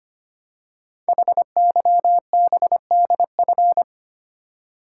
Morse code, heard as a keyed tone, about 25 wpm, 700 Hz. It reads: HYBDF
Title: Code Groups - 5 Letter:Numbers - 25wpm